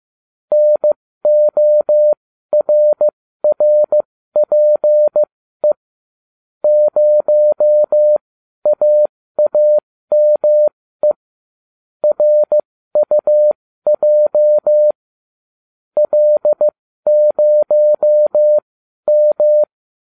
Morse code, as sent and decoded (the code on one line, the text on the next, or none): -. --- .-. .-. .--. / ----- .- .- -- . / .-. ..- .--- / .-.. ----- --
NORRPE 0AAME RUJ L0M